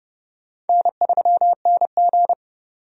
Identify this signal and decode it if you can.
Morse code, heard as a keyed tone, about 30 wpm, 700 Hz. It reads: N3NG
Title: USA Call Signs - 4 Characters - 30wpm